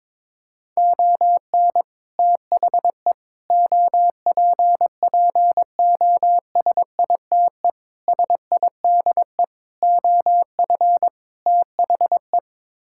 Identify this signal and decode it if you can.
Morse code, heard as a keyed tone, about 22 wpm, 700 Hz. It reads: ON THE OPPOSITE SIDE OF THE
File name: practice / 22wpm / Sets of 6 Words - 22wpm